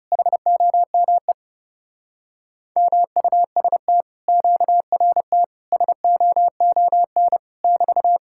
Morse code, heard as a keyed tone, about 30 wpm, 700 Hz. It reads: HOME MUST QRT SOON <BT>
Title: Speed-Racing Common QSO Phrases - 30wpm